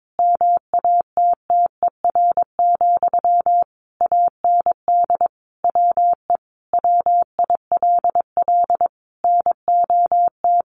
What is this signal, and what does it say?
Morse code, heard as a keyed tone, about 22 wpm, 700 Hz. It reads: MATTER, AND WE WILL NOT
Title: Sentences from Top 1100 Words 22wpm